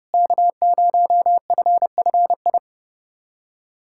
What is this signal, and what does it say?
Morse code, heard as a keyed tone, about 30 wpm, 700 Hz. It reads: K0FFI
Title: USA Callsign DE Callsign 30wpm